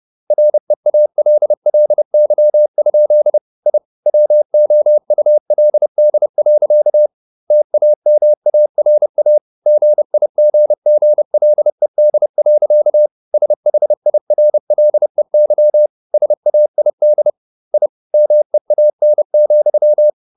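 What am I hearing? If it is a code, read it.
Morse code, heard as a keyed tone, about 30 wpm, 600 Hz. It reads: REALLY? I WOULD. TAMARA GIGGLED. SHIRLEY SAID I MEAN,
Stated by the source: Morse code at 30 wpm, the novel Never by Ken Follett in English, from CW med Gnister.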